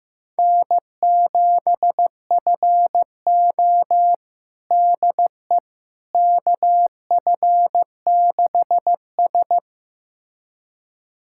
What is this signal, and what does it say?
Morse code, heard as a keyed tone, about 15 wpm, 700 Hz. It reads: N7FO DE KF6S
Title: USA Callsign DE Callsign - 4 Characters - 15wpm